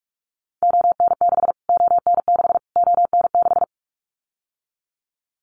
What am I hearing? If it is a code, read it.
Morse code, heard as a keyed tone, about 44 wpm, 700 Hz. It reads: ON6 ON6 ON6
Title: Callsign Prefixes for Top 25 DXCC - Repeated 3x - 45wpm